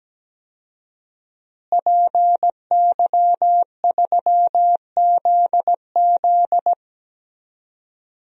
Morse code, heard as a keyed tone, about 17 wpm, 700 Hz. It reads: PY3ZZ
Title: DXCC-100 Call Signs - 5 Characters - Repeated 2x - 17wpm